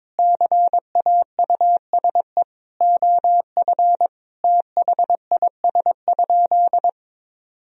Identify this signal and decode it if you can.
Morse code, heard as a keyed tone, about 22 wpm, 700 Hz. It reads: CAUSE OF THIS?
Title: Sentences from Top 1000 Words - Review Part 1 - 22wpm